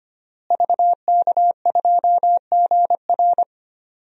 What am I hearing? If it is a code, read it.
Morse code, heard as a keyed tone, about 25 wpm, 700 Hz. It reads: VK2GR
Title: DXCC-100 Call Signs - 5 Characters - Repeated 3x - 25wpm